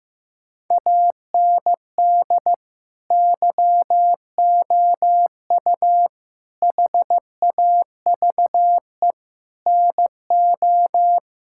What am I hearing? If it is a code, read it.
Morse code, heard as a keyed tone, about 15 wpm, 700 Hz. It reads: AND YOU HAVE NO
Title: Sentences from Top 1000 Words - Review Part 6 - 15wpm